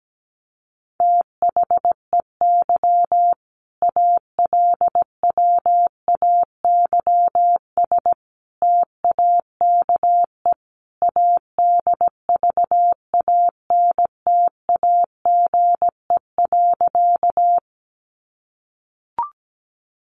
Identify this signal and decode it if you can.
Morse code, heard as a keyed tone, about 17 wpm, 700 Hz. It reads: THEY ALWAYS TAKE ADVANTAGE.
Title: Sentences from Top 900 Words 17wpm